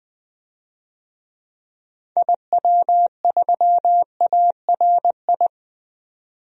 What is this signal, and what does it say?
Morse code, heard as a keyed tone, about 20 wpm, 700 Hz. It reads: IW3ARI
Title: DXCC-100 Call Signs - 6 Characters - 20wpm